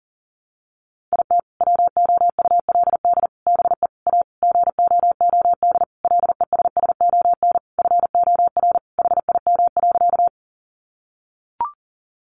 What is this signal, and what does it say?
Morse code, heard as a keyed tone, about 40 wpm, 700 Hz. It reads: IT WOULD BE A GOOD LESSON FOR HIM.
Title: Sentences from Top 1200 Words 40wpm